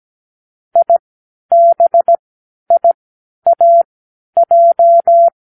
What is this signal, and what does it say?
Morse code, heard as a keyed tone, about 17 wpm, spaced as for 10 wpm, 680 Hz. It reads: IBIAJ
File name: k7qo / Lesson15(A-J)